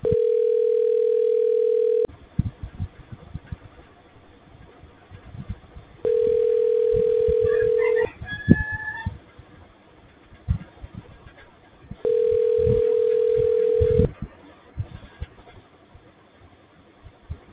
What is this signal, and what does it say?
Ringback tone (North America), 440 + 480 Hz.